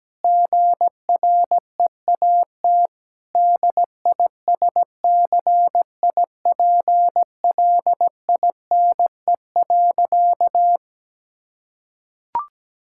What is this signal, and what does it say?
Morse code, heard as a keyed tone, about 17 wpm, 700 Hz. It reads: GREAT DISCIPLINE.
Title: Sentences from Top 1000 Words - Review Part 6 - 17wpm